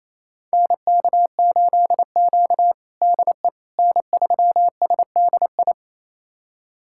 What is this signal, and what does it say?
Morse code, heard as a keyed tone, about 27 wpm, 700 Hz. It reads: NK8Q DE N3SDI
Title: USA Callsign DE Callsign 28wpm